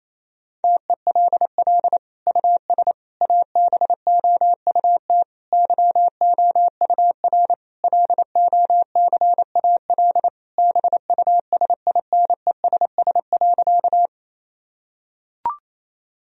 Morse code, heard as a keyed tone, about 28 wpm, 700 Hz. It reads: TELL US ABOUT YOUR LOCAL BUSINESS.
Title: Sentences from Top 1000 Words - Review Part 3 - 28wpm